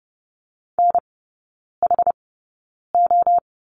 Morse code, heard as a keyed tone, about 30 wpm, spaced as for 9 wpm, 700 Hz. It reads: NHO
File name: Lesson 16e - Letter c - Three Characters 30wpm